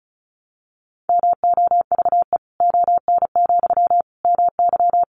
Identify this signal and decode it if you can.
Morse code, heard as a keyed tone, about 34 wpm, 700 Hz. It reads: MOVE ON, MY
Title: Sentences from Top 500 Words 35wpm